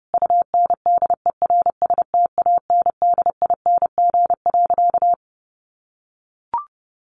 Morse code, heard as a keyed tone, about 30 wpm, 700 Hz. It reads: UNDERSTANDING.
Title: Sentences from Top 1400 Words 30wpm